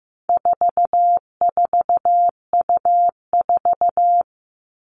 Morse code, heard as a keyed tone, about 15 wpm, 700 Hz. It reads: 44U4
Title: All Cut Numbers - 4-Digit 15wpm